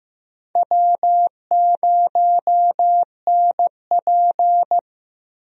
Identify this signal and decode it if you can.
Morse code, heard as a keyed tone, about 15 wpm, 700 Hz. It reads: W0NP
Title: USA Call Signs - 4 Characters - Repeated 2x - 15wpm